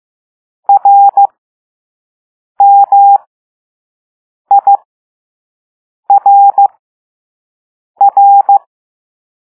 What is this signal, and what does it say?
Morse code, heard as a keyed tone, about 15 wpm, spaced as for 5 wpm, 800 Hz. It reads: RMIRR
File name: Morse / Level2A